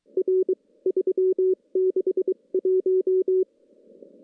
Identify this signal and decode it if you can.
Morse code, heard as a keyed tone, about 23 wpm, spaced as for 20 wpm, 380 Hz. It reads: R 361
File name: CW / CW-SS-k5tr- -1700